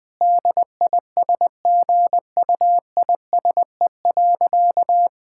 Morse code, heard as a keyed tone, about 21 wpm, 700 Hz. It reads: DISGUISE.
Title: Difficult Sentences with Hints - 20wpm